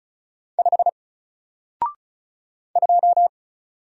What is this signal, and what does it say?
Morse code, heard as a keyed tone, about 36 wpm, 700 Hz. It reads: 5 2
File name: Single-Letter-Number-35wpm